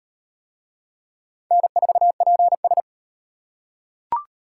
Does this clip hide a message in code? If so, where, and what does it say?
Yes: Morse code from 1.5 to 2.8 s, heard as a keyed tone, about 35 wpm, 700 Hz. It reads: N4PS